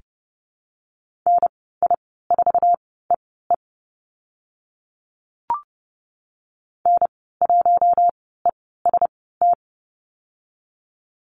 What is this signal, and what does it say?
Morse code, heard as a keyed tone, about 30 wpm, spaced as for 17 wpm, 700 Hz. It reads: NI4EE N1EST
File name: Lesson 14g - Number 2 - Call Signs 30wpm